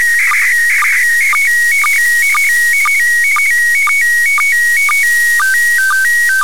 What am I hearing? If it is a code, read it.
SSTV (slow-scan television), PD 120.
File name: commit / satnogs 29412 short